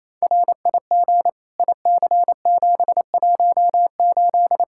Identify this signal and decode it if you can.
Morse code, heard as a keyed tone, about 28 wpm, 700 Hz. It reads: RIG IC718
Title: Practice QSO - Exchange 2 - Info - 28wpm